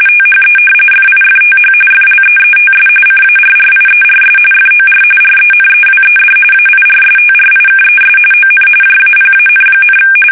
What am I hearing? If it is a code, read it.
RTTY, about 45.45 baud. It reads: THE FIFTH FLEETS CARRIERS FOR RENEWED AIRSTRIKES ON JAPAN,
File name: fsk850 / battleshipnj